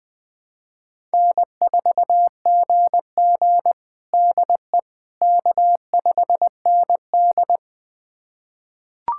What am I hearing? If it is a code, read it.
Morse code, heard as a keyed tone, about 20 wpm, 700 Hz. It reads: N4GG DE K5ND
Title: USA Callsign DE Callsign - 4 Characters - 20wpm